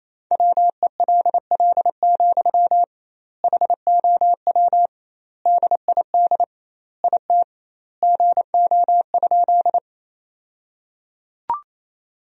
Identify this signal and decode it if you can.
Morse code, heard as a keyed tone, about 27 wpm, 700 Hz. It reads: WELL, HOW DID IT GO?
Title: 2x Word-Spacing - Sentences from Top 200 Words - 28wpm